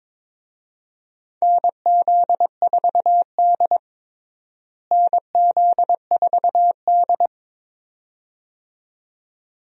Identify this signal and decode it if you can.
Morse code, heard as a keyed tone, about 22 wpm, 700 Hz. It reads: NZ4D NZ4D
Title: USA Call Signs - 4 Characters - Repeated 2x - 22wpm